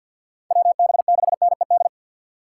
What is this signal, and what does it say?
Morse code, heard as a keyed tone, about 60 wpm, 700 Hz. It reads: WB6NED
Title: Speed Racing - USA 2x3 Call Signs 50wpm